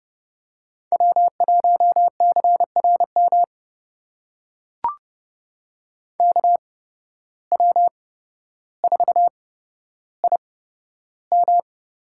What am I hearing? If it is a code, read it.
Morse code, heard as a keyed tone, about 31 wpm, spaced as for 13 wpm, 700 Hz. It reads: W1CRM KW4IM
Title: USA Call Signs - 5 Characters - 8x Character Spacing - 30wpm